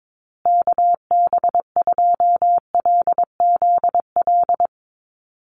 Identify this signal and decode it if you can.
Morse code, heard as a keyed tone, about 22 wpm, 700 Hz. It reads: KB2LZL
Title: USA Call Signs Repeated 3x 22wpm